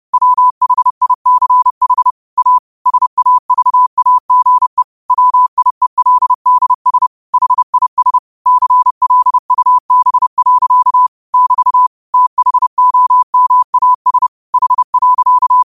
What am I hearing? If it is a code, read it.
Morse code, heard as a keyed tone, about 30 wpm, 1000 Hz. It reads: WHICH A SAVAGE WIELDS HIS CLUB. <BT> THOMAS H.
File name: qotdfiles / 30wpm